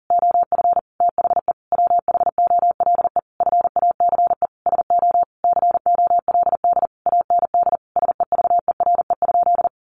Morse code, heard as a keyed tone, about 40 wpm, 700 Hz. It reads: OF THE WHOLE FACE SO COLD AND SEVERE?
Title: Sentences from Top 1400 Words 40wpm